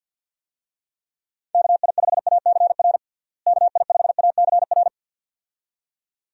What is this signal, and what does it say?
Morse code, heard as a keyed tone, about 46 wpm, 700 Hz. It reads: KI5ACR KI5ACR
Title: USA Call Signs - 6 Characters - Repeated 2x - 50wpm